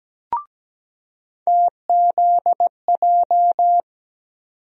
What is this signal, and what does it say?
Morse code, heard as a keyed tone, about 17 wpm, 700 Hz. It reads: TZJ
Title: USA Call Sign Postfixes - 3 Letters 17wpm